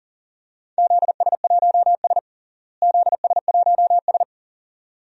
Morse code, heard as a keyed tone, about 40 wpm, 700 Hz. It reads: ZS1S ZS1S
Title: DXCC-100 Call Signs - 4 Characters - Repeated 2x - 40wpm